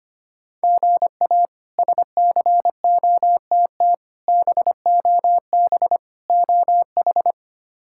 Morse code, heard as a keyed tone, about 25 wpm, 700 Hz. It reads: GA SCOTT BOB OH
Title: K1USN SST QSO Practice - 25wpm